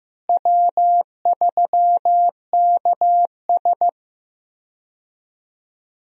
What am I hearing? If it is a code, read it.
Morse code, heard as a keyed tone, about 15 wpm, 700 Hz. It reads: W3KS
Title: USA Call Signs - 4 Characters - Repeated 3x - 15wpm